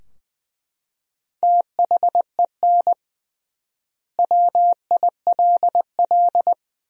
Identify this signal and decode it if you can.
Morse code, heard as a keyed tone, about 20 wpm, 700 Hz. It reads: THEN WILL
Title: 3x Word-Spacing - Sentences from Top 200 Words - 20wpm